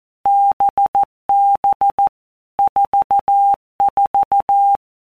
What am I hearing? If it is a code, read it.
Morse code, heard as a keyed tone, about 14 wpm, 800 Hz. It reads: BB 44
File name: Basic Letter Groups By Blocks / Groups@13 / VB46@13